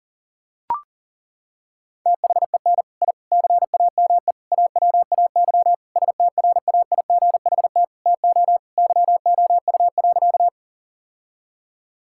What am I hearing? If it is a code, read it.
Morse code, heard as a keyed tone, about 40 wpm, 700 Hz. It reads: THEN I CAME AWAY STRAIGHT TO YOU.
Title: Sentences from Top 1000 Words - Review Part 6 - 40wpm